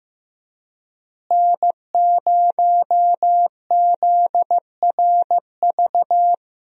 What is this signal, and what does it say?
Morse code, heard as a keyed tone, about 16 wpm, 700 Hz. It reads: N0ZRV
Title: USA Call Signs - 5 Characters - Repeated 3x - 15wpm